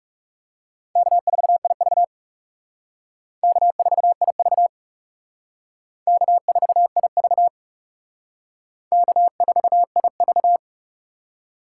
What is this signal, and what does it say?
Morse code, heard as a keyed tone, about 37 wpm, 700 Hz. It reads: K4IV K4IV K4IV K4IV
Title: Speed Racing - USA 1x2 Call Signs 30wpm